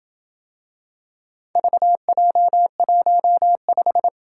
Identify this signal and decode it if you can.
Morse code, heard as a keyed tone, about 27 wpm, 700 Hz. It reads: VJ15